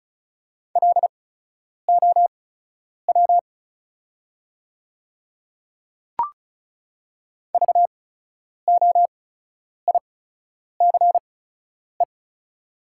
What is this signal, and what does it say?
Morse code, heard as a keyed tone, about 36 wpm, spaced as for 10 wpm, 690 Hz. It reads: LOW VOICE
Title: 8x Character Spacing - Top 500 Words - 35wpm